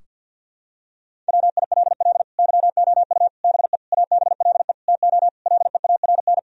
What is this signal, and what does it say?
Morse code, heard as a keyed tone, about 52 wpm, 700 Hz. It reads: WILL YOU BE ABLE TO LEARN
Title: Sentences from Top 1000 Words - Review Part 2 - 50wpm